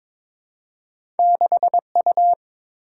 Morse code, heard as a keyed tone, about 22 wpm, 690 Hz. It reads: 6U
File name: practice / 22wpm / Two Letter-Number Combinations - 22wpm